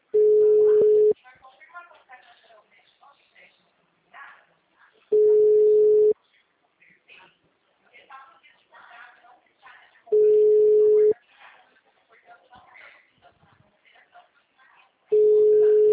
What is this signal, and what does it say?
Ringback tone (Europe, 425 Hz), with speech.